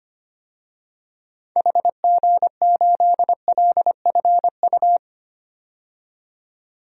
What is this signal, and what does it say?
Morse code, heard as a keyed tone, about 25 wpm, 690 Hz. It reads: HG8LFU
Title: DXCC-100 Call Signs - 6 Characters - 25wpm